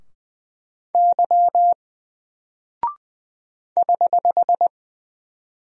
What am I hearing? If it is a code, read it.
Morse code, heard as a keyed tone, about 20 wpm, 700 Hz. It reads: Y <HH>
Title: Single Character (Letter:Number:Punctuation:Prosign) 20wpm